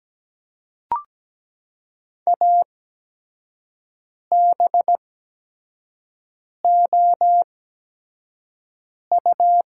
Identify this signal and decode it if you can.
Morse code, heard as a keyed tone, about 17 wpm, spaced as for 5 wpm, 700 Hz. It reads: ABOU
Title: 8x Character Spacing - Top 100 Words - 17wpm